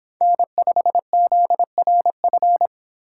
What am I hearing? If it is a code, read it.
Morse code, heard as a keyed tone, about 25 wpm, 700 Hz. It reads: N5ZRF